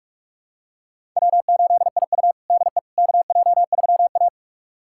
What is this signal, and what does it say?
Morse code, heard as a keyed tone, about 42 wpm, 700 Hz. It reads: W8IU DE KJ3A
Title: USA Callsign DE Callsign - 4 Characters - 45wpm